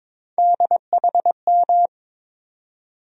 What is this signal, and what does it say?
Morse code, heard as a keyed tone, about 21 wpm, 700 Hz. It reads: DHM
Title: USA Call Sign Postfixes - 3 Letters 22wpm